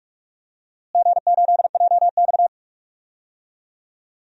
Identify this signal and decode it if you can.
Morse code, heard as a keyed tone, about 44 wpm, 700 Hz. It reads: G8JX
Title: DXCC-100 Call Signs - 4 Characters - Repeated 2x - 45wpm